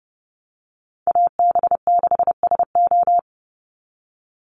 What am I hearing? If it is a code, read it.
Morse code, heard as a keyed tone, about 30 wpm, 700 Hz. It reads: AB6SO
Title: NA SOTA Chaser Callsigns 30wpm